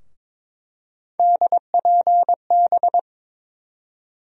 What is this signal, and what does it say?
Morse code, heard as a keyed tone, about 22 wpm, 700 Hz. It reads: DPB
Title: USA Call Sign Postfixes - 3 Letters 22wpm